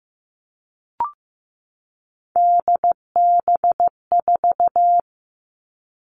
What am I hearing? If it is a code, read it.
Morse code, heard as a keyed tone, about 15 wpm, 700 Hz. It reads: DB4
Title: All Cut Numbers - 3-Digit 15wpm